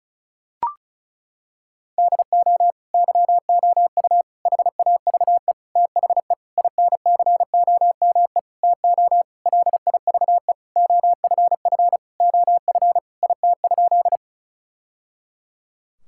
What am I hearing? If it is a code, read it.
Morse code, heard as a keyed tone, about 34 wpm, 700 Hz. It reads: DO YOU HAVE THE INCOME TO LIVE OFF OF IT?
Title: Sentences from Top 1000 Words - Review Part 5 - 35wpm